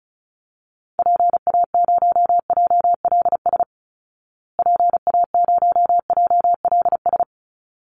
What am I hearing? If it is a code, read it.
Morse code, heard as a keyed tone, about 34 wpm, 700 Hz. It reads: PA0JLS PA0JLS
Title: Call Signs Repeated 3x - 35wpm